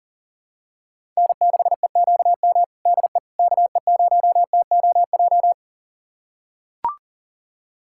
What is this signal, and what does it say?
Morse code, heard as a keyed tone, about 40 wpm, 700 Hz. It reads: N6EQM DE KE0TOJ
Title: USA Callsign DE Callsign 40wpm